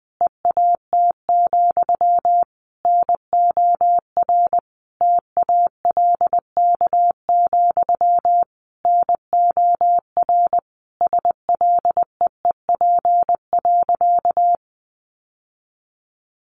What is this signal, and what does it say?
Morse code, heard as a keyed tone, about 20 wpm, 700 Hz. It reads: EAT, NOR TALK, NOR SLEEP.
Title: Sentences from Top 1500 Words 20wpm